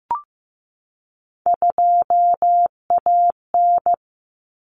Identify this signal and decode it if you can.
Morse code, heard as a keyed tone, about 15 wpm, 700 Hz. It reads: 2AN
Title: Common Cut Numbers - 3-Digit 15wpm